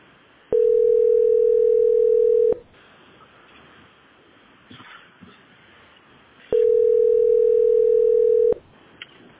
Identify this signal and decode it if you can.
Ringback tone (North America), 440 + 480 Hz.